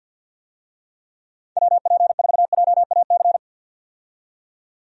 Morse code, heard as a keyed tone, about 50 wpm, 700 Hz. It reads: WP4JAC
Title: USA Call Signs - 6 Characters - 50wpm